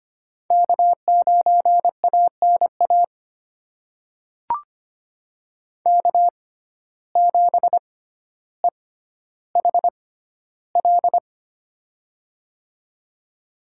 Morse code, heard as a keyed tone, about 25 wpm, spaced as for 13 wpm, 690 Hz. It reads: K9ANA K7EHL
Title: USA Call Signs - 5 Characters - 6x Character Spacing - 25wpm